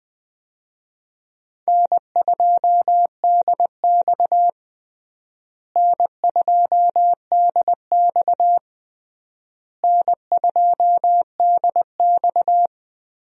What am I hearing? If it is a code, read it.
Morse code, heard as a keyed tone, about 20 wpm, 700 Hz. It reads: N2DX N2DX N2DX
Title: USA Call Signs - 4 Characters - Repeated 3x - 20wpm